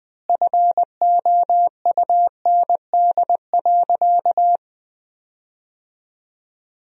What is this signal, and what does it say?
Morse code, heard as a keyed tone, about 20 wpm, 700 Hz. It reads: FOUND.
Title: 3x Word-Spacing - Sentences from Top 300 Words - 20wpm